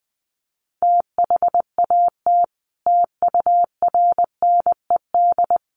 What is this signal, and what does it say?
Morse code, heard as a keyed tone, about 20 wpm, 700 Hz. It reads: THAT TURNED
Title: Sets of 5 Words Encore2 20wpm